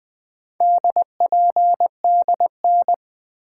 Morse code, heard as a keyed tone, about 21 wpm, 700 Hz. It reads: DPDN
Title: Four Letter-Number Combinations - 20wpm